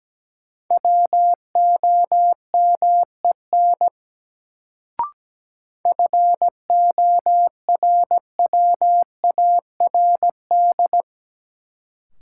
Morse code, heard as a keyed tone, about 17 wpm, 700 Hz. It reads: WOMEN FORWARD
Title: Sentences from Top 1300 Words - New Word Drill - 17wpm